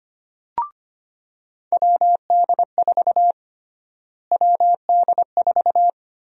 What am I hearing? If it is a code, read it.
Morse code, heard as a keyed tone, about 25 wpm, 700 Hz. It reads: WD4 WD4
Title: USA Call Sign Prefixes - Repeated 2x - 25wpm